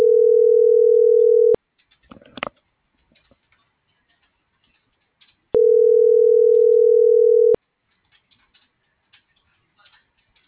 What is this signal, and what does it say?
Ringback tone (North America), 440 + 480 Hz.